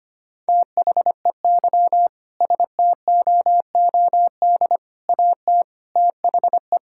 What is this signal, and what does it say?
Morse code, heard as a keyed tone, about 25 wpm, 700 Hz. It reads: THEY STOOD AT THE
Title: Sentences from Top 1600 Words 25wpm